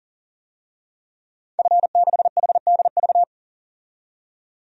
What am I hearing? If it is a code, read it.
Morse code, heard as a keyed tone, about 37 wpm, 700 Hz. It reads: F6HDV
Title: DXCC-100 Call Signs - 5 Characters - 40wpm